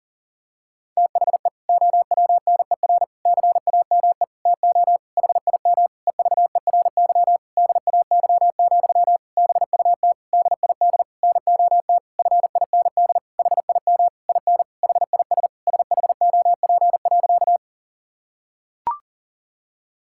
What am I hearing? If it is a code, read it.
Morse code, heard as a keyed tone, about 40 wpm, 700 Hz. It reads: THE OWNER CAME TO HIM EVERY DAY, BUT DID NOT FIND HIM IN HIS SHOP.
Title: Sentences from Top 1600 Words 40wpm